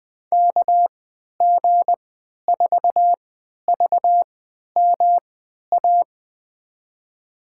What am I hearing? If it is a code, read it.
Morse code, heard as a keyed tone, about 20 wpm, spaced as for 11 wpm, 700 Hz. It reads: KG4VMA
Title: USA Call Signs - 6 Characters - 3x Character Spacing - 20wpm